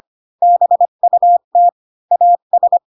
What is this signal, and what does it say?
Morse code, heard as a keyed tone, about 25 wpm, 700 Hz. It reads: BUT AS
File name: Pinocchio / Pinocchio ch1 20 25 ik1hge